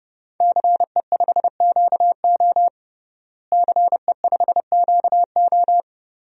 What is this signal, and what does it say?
Morse code, heard as a keyed tone, about 32 wpm, 700 Hz. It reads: CE5QO CE5QO
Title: DXCC-100 Call Signs - 5 Characters - Repeated 2x - 30wpm